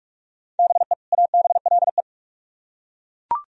Morse code, heard as a keyed tone, about 41 wpm, 700 Hz. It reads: BE ABLE